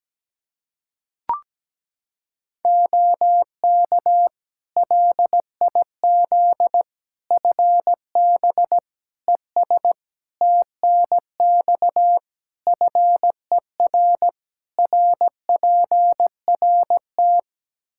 Morse code, heard as a keyed tone, about 17 wpm, 700 Hz. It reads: OK LIZ FB ES TNX FER RPRT
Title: K1USN SST QSO Practice 17wpm